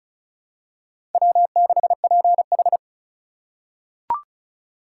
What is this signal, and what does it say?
Morse code, heard as a keyed tone, about 36 wpm, 700 Hz. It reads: W6PH
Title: USA Call Signs - 4 Characters - 4x Character Spacing - 35wpm